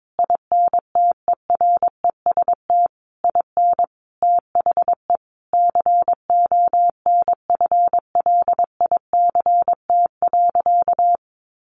Morse code, heard as a keyed tone, about 22 wpm, 700 Hz. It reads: INTEREST IN THE CONFLICT.
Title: Sentences from Top 1300 Words 22wpm